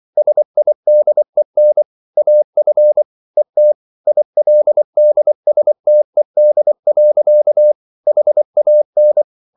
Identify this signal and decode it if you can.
Morse code, heard as a keyed tone, about 24 wpm, 600 Hz. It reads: SIDEN AF ET ILDSTED. HAN
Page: Ferie 24 wpm